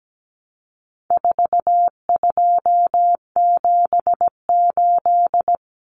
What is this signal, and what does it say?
Morse code, heard as a keyed tone, about 17 wpm, 700 Hz. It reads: 4278
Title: Numbers - 4-Digit 17wpm